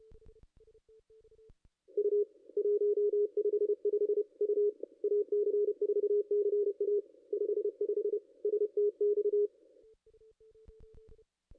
Morse code, heard as a keyed tone, about 29 wpm, 440 Hz. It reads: U 155 U AC4CA 55 STX
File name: contests / ss-cw-k5tr-op-n5rz -2200